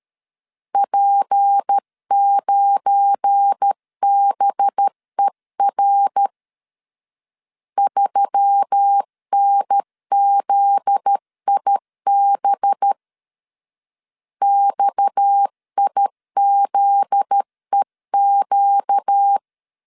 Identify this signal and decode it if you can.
Morse code, heard as a keyed tone, about 13 wpm, 800 Hz. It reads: P9BER 3NZIB XIZEQ
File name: LEZIONI / 7 GF9Z / 7 GF9Z 12wpm 2147